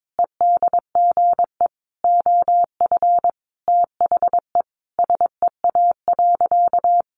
Morse code, heard as a keyed tone, about 22 wpm, 700 Hz. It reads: EDGE OF THE SEA.
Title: Sentences from Top 1300 Words 22wpm